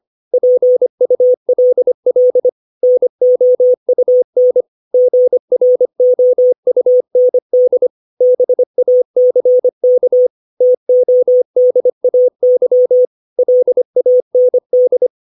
Morse code, heard as a keyed tone, about 25 wpm, 500 Hz. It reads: PULL NOUN GROUND BACK TODAY LAND
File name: 632 words / 632 Words 25WPM